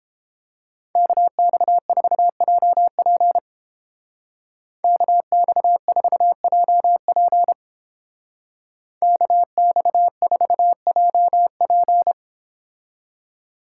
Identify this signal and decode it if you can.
Morse code, heard as a keyed tone, about 31 wpm, 700 Hz. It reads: KX4JP KX4JP KX4JP